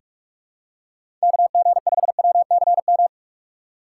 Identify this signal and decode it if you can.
Morse code, heard as a keyed tone, about 44 wpm, 700 Hz. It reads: KG5WCM
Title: USA Call Signs - 6 Characters - 6x Character Spacing - 45wpm